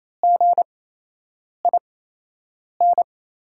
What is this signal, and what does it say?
Morse code, heard as a keyed tone, about 28 wpm, spaced as for 8 wpm, 700 Hz. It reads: GIN
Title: 8x Character Spacing - Top 500 Words - 28wpm